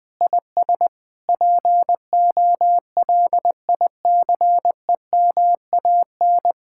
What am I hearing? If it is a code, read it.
Morse code, heard as a keyed tone, about 20 wpm, 700 Hz. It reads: IS POLICEMAN
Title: Speed-Racing Common QSO Phrases - 17wpm